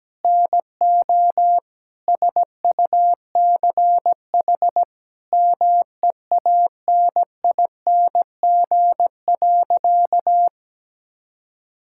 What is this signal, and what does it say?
Morse code, heard as a keyed tone, about 17 wpm, 700 Hz. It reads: NO SUCH MEANING.
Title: Sentences from Top 1000 Words - Review Part 6 - 17wpm